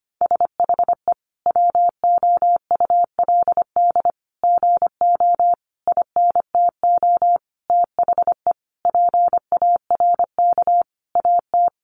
Morse code, heard as a keyed tone, about 25 wpm, 700 Hz. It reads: SHE WOULD GO INTO THE PARK AT
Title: Sentences from Top 1300 Words 25wpm